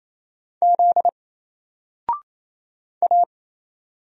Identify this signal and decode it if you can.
Morse code, heard as a keyed tone, about 27 wpm, 700 Hz. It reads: Z A
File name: Single-Letter-Number-28wpm